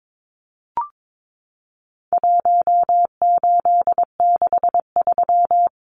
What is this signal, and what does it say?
Morse code, heard as a keyed tone, about 22 wpm, 700 Hz. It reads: 1863
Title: Numbers - Contest Style - 4-Digit 22wpm